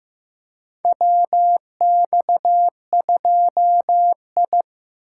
Morse code, heard as a keyed tone, about 15 wpm, 700 Hz. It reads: WX2I
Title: USA Call Signs - 4 Characters - 6x Character Spacing - 15wpm